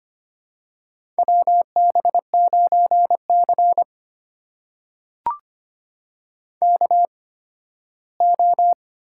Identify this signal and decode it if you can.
Morse code, heard as a keyed tone, about 25 wpm, spaced as for 15 wpm, 700 Hz. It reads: WB9C KO
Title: USA Call Signs - 4 Characters - 8x Character Spacing - 25wpm